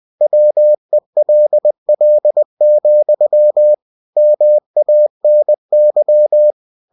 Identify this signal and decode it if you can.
Morse code, heard as a keyed tone, about 20 wpm, 600 Hz. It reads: WELL, MANY
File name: cw-books / winnie-the-pooh / winnie-20wpm-page-0047